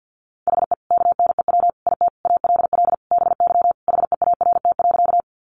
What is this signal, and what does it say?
Morse code, heard as a keyed tone, about 50 wpm, 700 Hz. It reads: HE KNEW IT ALL BY HEART.
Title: Sentences from Top 1200 Words 50wpm